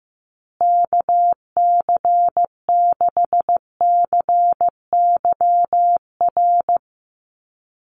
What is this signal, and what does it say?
Morse code, heard as a keyed tone, about 15 wpm, 700 Hz. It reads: KC6CYR
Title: USA Call Signs Repeated 2x 15wpm